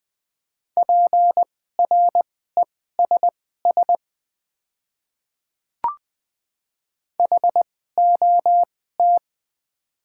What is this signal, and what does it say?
Morse code, heard as a keyed tone, about 20 wpm, spaced as for 15 wpm, 690 Hz. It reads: PRESS HOT
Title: 2x Character Spacing - Top 500 Words - 20wpm